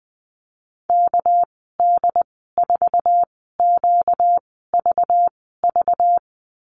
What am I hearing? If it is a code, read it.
Morse code, heard as a keyed tone, about 20 wpm, spaced as for 15 wpm, 700 Hz. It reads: KD4QVV
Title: 238 Lesson 38g - Letter q - Call Signs 20wpm